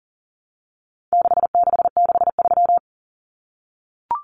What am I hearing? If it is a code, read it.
Morse code, heard as a keyed tone, about 39 wpm, 700 Hz. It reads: 6663
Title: Numbers - 4-Digit 40wpm